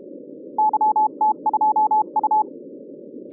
Morse code, heard as a keyed tone, about 32 wpm, 860 Hz. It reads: YT2U
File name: 2011 bcc-challenge sonntag tape